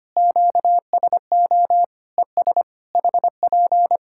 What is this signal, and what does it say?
Morse code, heard as a keyed tone, about 25 wpm, 700 Hz. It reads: QSO ES HP
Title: Practice QSO - Exchange 4 - Ending - 25wpm